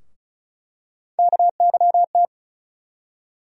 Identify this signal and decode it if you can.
Morse code, heard as a keyed tone, about 36 wpm, 700 Hz. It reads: KYT
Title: USA Call Sign Postfixes - 3 Letters 35wpm